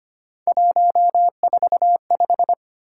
Morse code, heard as a keyed tone, about 25 wpm, 700 Hz. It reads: 145
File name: Three Letter-Number Combinations - 25wpm